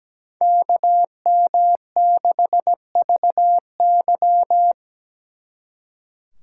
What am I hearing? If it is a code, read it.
Morse code, heard as a keyed tone, about 19 wpm, 700 Hz. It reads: KM6VY
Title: Speed Racing - USA Call Signs 17wpm